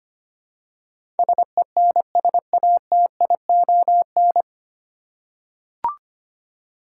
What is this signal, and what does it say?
Morse code, heard as a keyed tone, about 25 wpm, 700 Hz. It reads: SENSATION
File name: Nine Letter Words - 25wpm